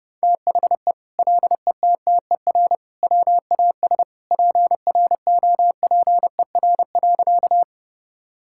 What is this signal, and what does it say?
Morse code, heard as a keyed tone, about 30 wpm, 700 Hz. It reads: THE LETTER WAS PROPER.
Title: Sentences from Top 1000 Words - Review Part 5 - 30wpm